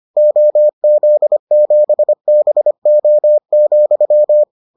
Morse code, heard as a keyed tone, about 25 wpm, 600 Hz. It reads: OZ7BO,